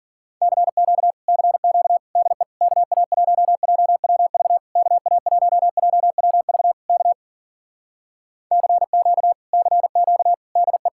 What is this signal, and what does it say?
Morse code, heard as a keyed tone, about 52 wpm, 700 Hz. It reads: CQ CQ DE KA1JWV KA1JWV K CQ CQ DE